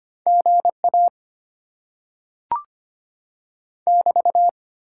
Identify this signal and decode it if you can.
Morse code, heard as a keyed tone, about 25 wpm, 700 Hz. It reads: GA <BT>
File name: Top 90 QSO Elements - 8x Character Spacing - 25wpm